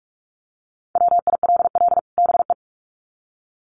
Morse code, heard as a keyed tone, about 44 wpm, 700 Hz. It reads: WILL BE
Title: Sets of 2 Words 45wpm